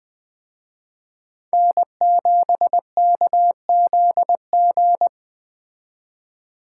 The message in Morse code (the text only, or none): N7KZG